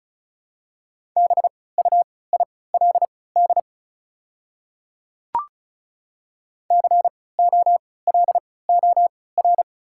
Morse code, heard as a keyed tone, about 34 wpm, spaced as for 20 wpm, 700 Hz. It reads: BUILD COLOR
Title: 3x Character Spacing - Top 300 Words - 35wpm